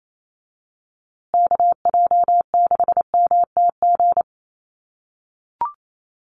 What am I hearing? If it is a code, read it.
Morse code, heard as a keyed tone, about 28 wpm, 700 Hz. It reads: KJ6MTG
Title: USA Call Signs Repeated 3x 28wpm